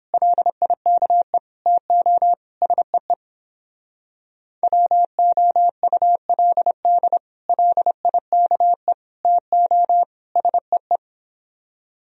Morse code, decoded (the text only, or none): LIKE TO SEE WOULD LIKE TO SEE